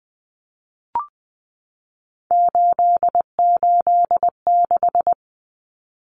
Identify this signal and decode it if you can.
Morse code, heard as a keyed tone, about 20 wpm, 700 Hz. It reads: 886
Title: Numbers - 3-Digit 20wpm